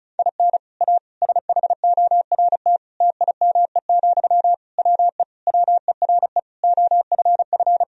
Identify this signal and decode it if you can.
Morse code, heard as a keyed tone, about 36 wpm, 700 Hz. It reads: IN A SHORT TIME, WE WERE OFF
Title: Sentences from Top 1000 Words - Review Part 2 - 35wpm